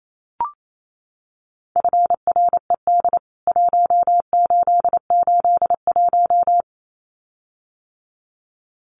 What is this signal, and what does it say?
Morse code, heard as a keyed tone, about 28 wpm, 700 Hz. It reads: FRED 1881
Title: CWT Exchange 28wpm